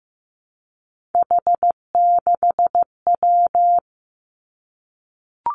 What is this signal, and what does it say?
Morse code, heard as a keyed tone, about 15 wpm, 700 Hz. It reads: H6W
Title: Three Challenging Letter-Number Combinations 15wpm